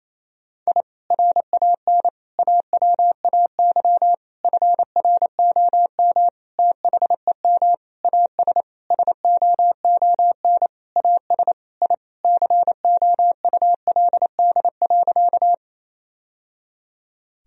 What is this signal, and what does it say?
Morse code, heard as a keyed tone, about 28 wpm, 700 Hz. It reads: I RAN AWAY FROM THEM AS SOON AS I COULD.
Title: Sentences from Top 1000 Words - Review Part 3 - 28wpm